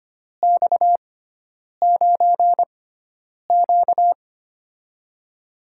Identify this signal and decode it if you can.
Morse code, heard as a keyed tone, about 25 wpm, spaced as for 9 wpm, 700 Hz. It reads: X9Q
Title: USA Call Signs - 6x Character Spacing - 25wpm